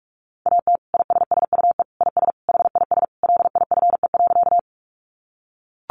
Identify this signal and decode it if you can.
Morse code, heard as a keyed tone, about 44 wpm, 700 Hz. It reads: AT ISSUE IS HIS LIFE.
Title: Sentences from Top 900 Words 45wpm